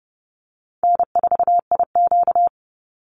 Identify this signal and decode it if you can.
Morse code, heard as a keyed tone, about 30 wpm, 700 Hz. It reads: N4IQ
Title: Call Signs - 30wpm